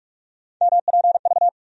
Morse code, heard as a keyed tone, about 47 wpm, 700 Hz. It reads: MPV
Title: USA Call Sign Postfixes - 3 Letters 45wpm